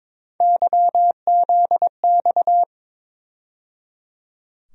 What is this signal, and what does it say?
Morse code, heard as a keyed tone, about 22 wpm, 700 Hz. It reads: YZX
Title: USA Call Sign Postfixes - 3 Letters 22wpm